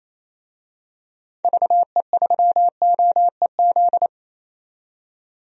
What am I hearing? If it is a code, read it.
Morse code, heard as a keyed tone, about 28 wpm, 700 Hz. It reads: VE3OEZ
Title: DXCC-100 Call Signs - 6 Characters - 28wpm